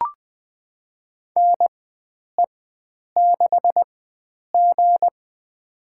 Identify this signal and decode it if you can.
Morse code, heard as a keyed tone, about 20 wpm, spaced as for 9 wpm, 700 Hz. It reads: NE6G
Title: USA Call Signs - 4 Characters - 4x Character Spacing - 20wpm